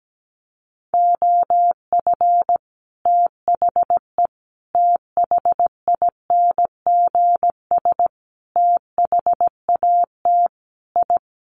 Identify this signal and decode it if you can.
Morse code, heard as a keyed tone, about 17 wpm, 700 Hz. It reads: OF THE THINGS THAT I
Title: Sets of 5 Words Encore 17wpm